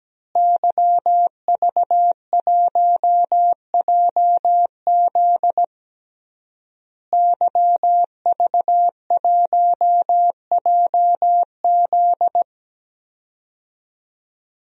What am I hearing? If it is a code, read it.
Morse code, heard as a keyed tone, about 17 wpm, 700 Hz. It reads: YV1JZ YV1JZ
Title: DXCC-100 Call Signs - 5 Characters - Repeated 3x - 17wpm